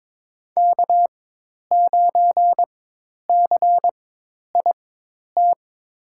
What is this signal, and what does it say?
Morse code, heard as a keyed tone, about 22 wpm, spaced as for 10 wpm, 700 Hz. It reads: K9CIT
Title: USA Call Signs - 5 Characters - 4x Character Spacing - 22wpm